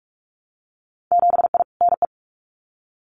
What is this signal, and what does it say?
Morse code, heard as a keyed tone, about 46 wpm, 690 Hz. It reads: 7I NE
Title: Winter Field Day Exchanges Encore 45wpm